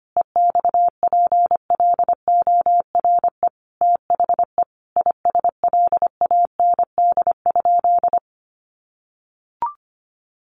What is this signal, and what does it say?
Morse code, heard as a keyed tone, about 25 wpm, 700 Hz. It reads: EXPLORE THE ISLAND?
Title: Sentences from Top 1100 Words 25wpm